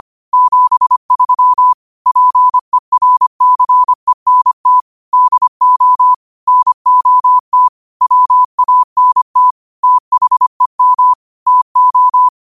Morse code, heard as a keyed tone, about 25 wpm, 1000 Hz. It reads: 73 PERCENT DO NOT WANT THEM TO
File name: qotdfiles / 25wpm